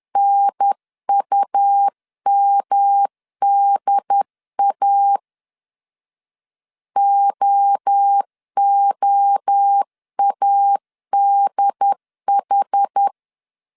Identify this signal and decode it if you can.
Morse code, heard as a keyed tone, about 10 wpm, 800 Hz. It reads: NUMDA OOADH
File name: 10wpm